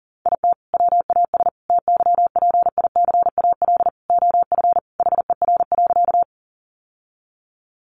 Morse code, heard as a keyed tone, about 40 wpm, 700 Hz. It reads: IT WAS TYPICAL OF HER.
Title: Sentences from Top 1100 Words 40wpm